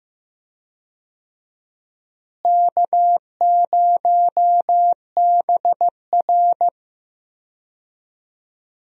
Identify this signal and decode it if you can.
Morse code, heard as a keyed tone, about 16 wpm, 700 Hz. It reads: K0BR
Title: USA Call Signs - 4 Characters - Repeated 2x - 15wpm